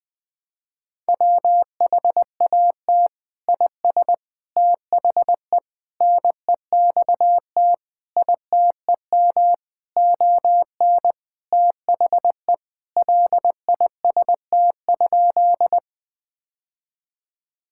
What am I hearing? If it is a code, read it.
Morse code, heard as a keyed tone, about 20 wpm, 700 Hz. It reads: WHAT IS THE NEXT ITEM ON THE LIST?
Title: Sentences from Top 1000 Words - Review Part 6 - 20wpm